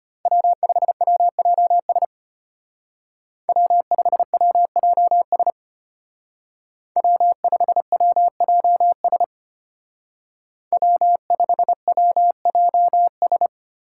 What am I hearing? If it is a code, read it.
Morse code, heard as a keyed tone, about 30 wpm, 700 Hz. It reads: W5WJS W5WJS W5WJS W5WJS
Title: Speed Racing - USA 1x3 Call Signs 25wpm